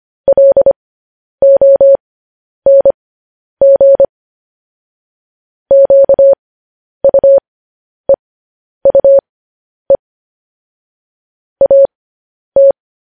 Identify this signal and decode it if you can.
Morse code, heard as a keyed tone, about 25 wpm, spaced as for 10 wpm, 550 Hz. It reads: LONG QUEUE AT
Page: news headlines in morse code at 10 wpm